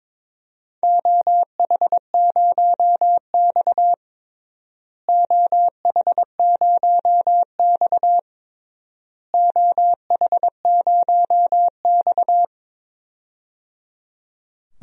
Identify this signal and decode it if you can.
Morse code, heard as a keyed tone, about 22 wpm, 700 Hz. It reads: OH0X OH0X OH0X
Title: DXCC-100 Call Signs - 4 Characters - Repeated 3x - 22wpm